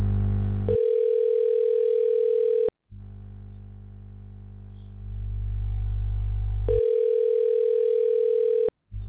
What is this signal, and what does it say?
Ringback tone (North America), 440 + 480 Hz.
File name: recordings / RE481c69e08e853fe60c5c16af